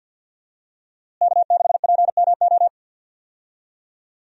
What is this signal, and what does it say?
Morse code, heard as a keyed tone, about 51 wpm, 700 Hz. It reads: K6PMO
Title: USA Call Signs - 5 Characters - Repeated 2x - 50wpm